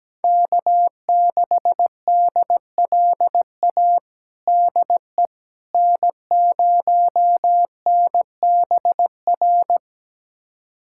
Morse code, heard as a keyed tone, about 17 wpm, 700 Hz. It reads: K6DLA DE N0NBR
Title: USA Callsign DE Callsign - 5 Characters - 17wpm